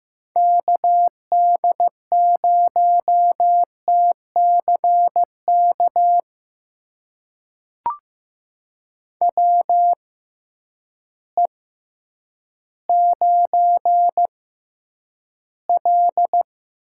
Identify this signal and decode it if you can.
Morse code, heard as a keyed tone, about 15 wpm, spaced as for 9 wpm, 700 Hz. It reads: KD0TCK WE9L
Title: USA Call Signs - 6x Character Spacing - 15wpm